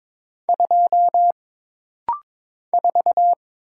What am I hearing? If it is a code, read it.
Morse code, heard as a keyed tone, about 22 wpm, 700 Hz. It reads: 2 4
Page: This rapid-fire timing is geared to push your ICR (Instant Character Recognition) skills.